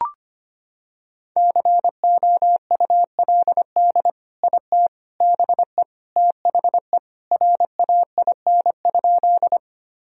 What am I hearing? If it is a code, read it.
Morse code, heard as a keyed tone, about 25 wpm, 700 Hz. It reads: COULD IT BE THE RAIN?
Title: Sentences from Top 1000 Words - Review Part 1 - 25wpm